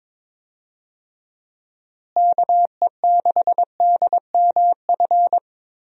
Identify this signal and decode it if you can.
Morse code, heard as a keyed tone, about 22 wpm, 700 Hz. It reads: KE6DMF
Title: USA Call Signs - 6 Characters - Repeated 3x - 22wpm